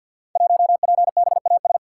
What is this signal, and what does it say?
Morse code, heard as a keyed tone, about 48 wpm, 700 Hz. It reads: 1PBAS
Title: Code Groups - 5 Letter:Numbers - 50wpm